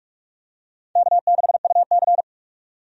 Morse code, heard as a keyed tone, about 46 wpm, 700 Hz. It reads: K6UC
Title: USA Call Signs - 4 Characters - 45wpm